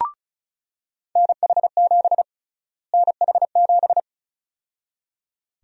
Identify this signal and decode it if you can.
Morse code, heard as a keyed tone, about 34 wpm, 700 Hz. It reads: NH7 NH7
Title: USA Call Sign Prefixes - Repeated 2x - 35wpm